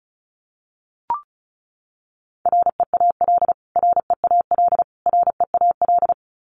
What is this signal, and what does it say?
Morse code, heard as a keyed tone, about 34 wpm, 700 Hz. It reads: REAL REAL REAL
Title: Top 500 Words Repeated 35wpm